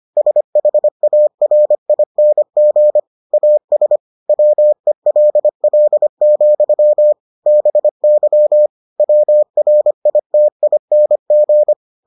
Morse code, heard as a keyed tone, about 25 wpm, 600 Hz. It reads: SHARING AS WELL, BY WRITING